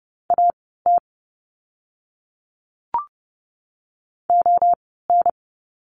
Morse code, heard as a keyed tone, about 30 wpm, spaced as for 17 wpm, 700 Hz. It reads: AT ON
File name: course / 30:10wpm / 230 Lesson 37f - Letter x - Words 30wpm